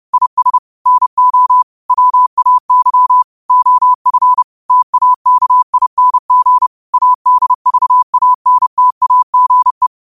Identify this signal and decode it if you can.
Morse code, heard as a keyed tone, about 30 wpm, 1000 Hz. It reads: IS NO WAY OF TAKING ADVANTAGE
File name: qotdfiles / 30wpm